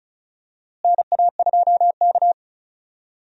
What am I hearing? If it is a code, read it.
Morse code, heard as a keyed tone, about 36 wpm, 700 Hz. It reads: NA2K
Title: USA Call Signs - 4 Characters - Repeated 2x - 35wpm